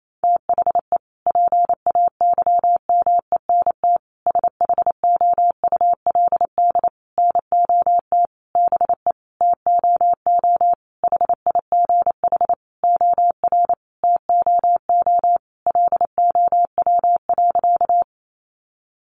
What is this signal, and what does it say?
Morse code, heard as a keyed tone, about 28 wpm, 700 Hz. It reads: THE PAYMENT SHOULD NOT BE TOO HIGH OR TOO LOW.
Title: Sentences from Top 1200 Words 28wpm